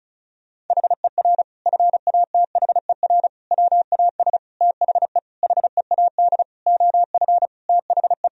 Morse code, heard as a keyed tone, about 34 wpm, 700 Hz. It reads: HER FATHER WAS THE HEAD OF THE
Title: Sentences from Top 1000 Words - Review Part 4 - 35wpm